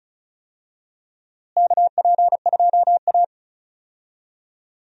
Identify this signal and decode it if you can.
Morse code, heard as a keyed tone, about 36 wpm, 700 Hz. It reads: KP2A
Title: DXCC-100 Call Signs - 4 Characters - Repeated 3x - 35wpm